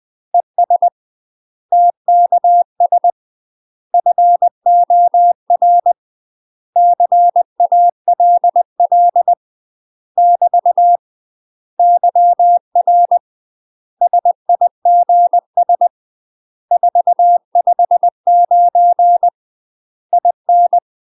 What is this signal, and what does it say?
Morse code, heard as a keyed tone, about 20 wpm, 700 Hz. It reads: ES TKS FOR CALL <BT> YR SIGS 459 IN
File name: UK MASTER QSOs TRAINING / 20WPM / UK MASTER QSOs 0035